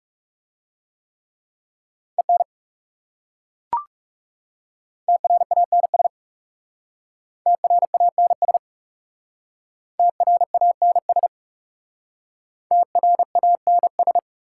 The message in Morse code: EN TRANS TRANS TRANS TRANS

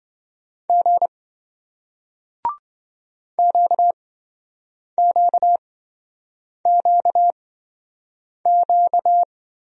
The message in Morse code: G QQQQ

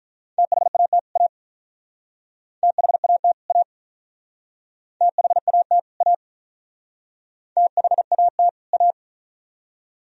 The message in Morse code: THAT A THAT A THAT A THAT A

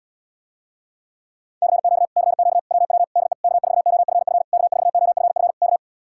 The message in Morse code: CQ CQ WW DE K4OFF K4OFF K

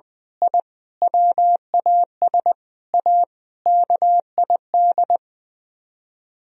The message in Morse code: I WAS A KID

